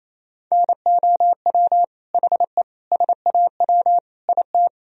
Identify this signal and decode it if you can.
Morse code, heard as a keyed tone, about 28 wpm, 700 Hz. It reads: NOW HE SAW IT
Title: Sentences from Top 1000 Words - Review Part 4 - 28wpm